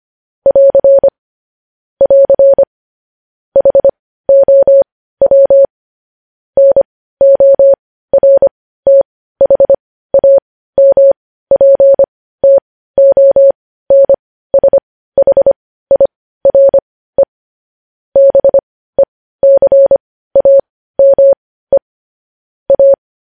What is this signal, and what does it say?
Morse code, heard as a keyed tone, about 25 wpm, spaced as for 15 wpm, 550 Hz. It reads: <AR> <AR> HOW NORTHAMPTONSHIRE BECAME A
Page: news headlines in morse code at 15 wpm